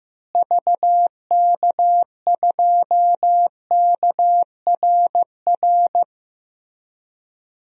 Morse code, heard as a keyed tone, about 15 wpm, 700 Hz. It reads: VK2KRR
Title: DXCC-100 Call Signs - 6 Characters - Repeated 2x - 15wpm